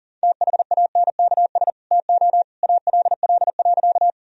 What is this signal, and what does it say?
Morse code, heard as a keyed tone, about 40 wpm, 700 Hz. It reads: THANKS TO ALL.
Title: Sentences from Top 1000 Words - Review Part 3 - 40wpm